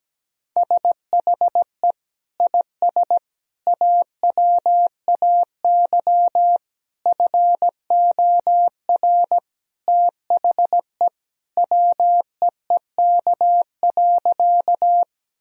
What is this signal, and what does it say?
Morse code, heard as a keyed tone, about 17 wpm, 700 Hz. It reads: SHE IS AWAY FOR THE WEEK.
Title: Sentences from Top 1000 Words - Review Part 2 - 17wpm